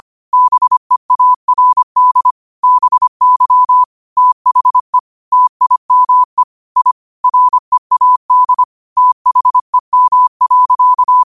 Morse code, heard as a keyed tone, about 25 wpm, 1000 Hz. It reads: BEARD BY THE TIME I READ THEM.
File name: qotdfiles / 25wpm